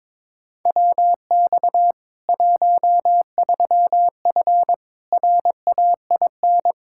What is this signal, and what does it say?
Morse code, heard as a keyed tone, about 22 wpm, 700 Hz. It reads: WX 13F RAIN
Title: Practice QSO - Exchange 2 - Info - 22wpm